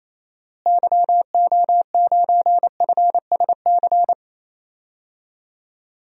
Morse code, heard as a keyed tone, about 28 wpm, 700 Hz. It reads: YO9FSC
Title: DXCC-100 Call Signs - 6 Characters - Repeated 2x - 28wpm